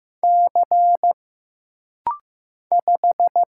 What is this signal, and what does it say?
Morse code, heard as a keyed tone, about 15 wpm, 700 Hz. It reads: C 5
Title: Single Letter-Number Alt - Rapid-Fire 15wpm